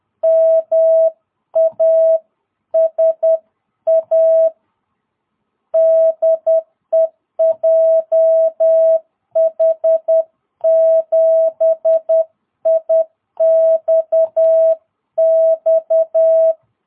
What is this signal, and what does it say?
Morse code, heard as a keyed tone, about 10 wpm, 660 Hz. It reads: MASA DE JH7IXX
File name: jh8cbh / 2022cw